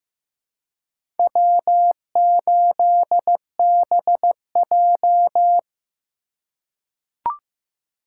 Morse code, heard as a keyed tone, about 15 wpm, 700 Hz. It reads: W8BJ
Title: USA Call Signs - 4 Characters - 2x Character Spacing - 15wpm